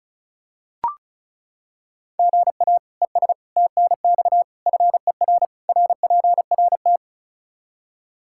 Morse code, heard as a keyed tone, about 34 wpm, 700 Hz. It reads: GA ES TNX FER RPRT
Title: K1USN SST QSO Practice 35wpm